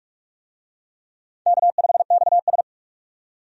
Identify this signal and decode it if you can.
Morse code, heard as a keyed tone, about 46 wpm, 700 Hz. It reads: K5XS
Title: USA Call Signs - 4 Characters - 45wpm